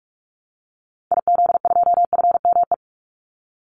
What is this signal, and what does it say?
Morse code, heard as a keyed tone, about 44 wpm, 700 Hz. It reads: IZ2FME
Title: Call Signs Repeated 3x - 45wpm